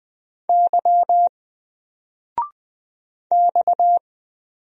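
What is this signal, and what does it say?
Morse code, heard as a keyed tone, about 20 wpm, 700 Hz. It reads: Y X
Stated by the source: Single letter and number Morse drill at 20 wpm, from N4LCD.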